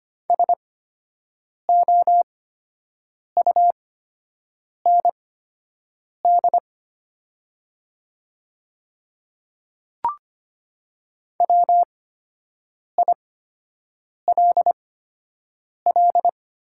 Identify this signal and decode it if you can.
Morse code, heard as a keyed tone, about 25 wpm, spaced as for 7 wpm, 700 Hz. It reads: SOUND WILL
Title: 8x Character Spacing - Top 100 Words - 25wpm